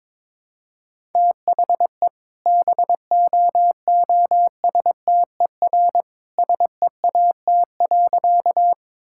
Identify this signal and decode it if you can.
Morse code, heard as a keyed tone, about 22 wpm, 690 Hz. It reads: THE BOOSTER SEAT.
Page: morse-code-ninja-practice-22wpm